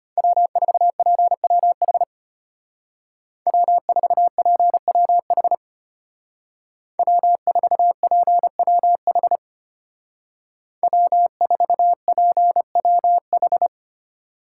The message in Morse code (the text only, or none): W4PWH W4PWH W4PWH W4PWH